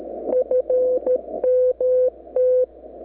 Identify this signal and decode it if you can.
Morse code, heard as a keyed tone, about 13 wpm, 510 Hz. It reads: FMT
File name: w6oqi 9-11-11-40M-SNIP